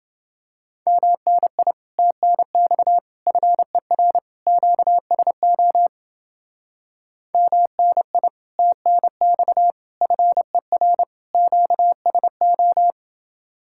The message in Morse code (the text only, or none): MNI TNX FER QSO MNI TNX FER QSO